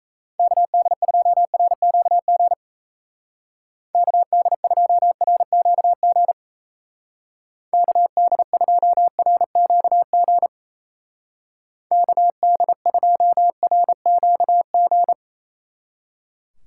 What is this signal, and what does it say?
Morse code, heard as a keyed tone, about 34 wpm, 700 Hz. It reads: KD2RQG KD2RQG KD2RQG KD2RQG
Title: Speed Racing - USA 2x3 Call Signs 28wpm